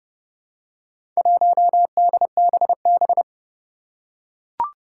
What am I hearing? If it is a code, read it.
Morse code, heard as a keyed tone, about 30 wpm, 700 Hz. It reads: 1DBB
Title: Four Challenging Letter-Number Combinations 30wpm